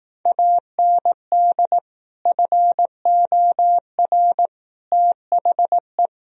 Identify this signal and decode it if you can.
Morse code, heard as a keyed tone, about 18 wpm, 700 Hz. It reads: AND FOR THE
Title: Speed-Racing - Sets of 5 Words 15wpm